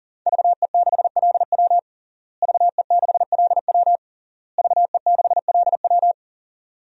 Morse code, heard as a keyed tone, about 40 wpm, 700 Hz. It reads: VE6LW VE6LW VE6LW
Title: DXCC-100 Call Signs - 5 Characters - Repeated 3x - 40wpm